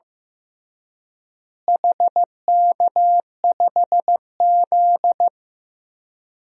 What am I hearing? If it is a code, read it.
Morse code, heard as a keyed tone, about 16 wpm, 700 Hz. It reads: HK5Z
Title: DXCC-100 Call Signs - 4 Characters - Repeated 2x - 15wpm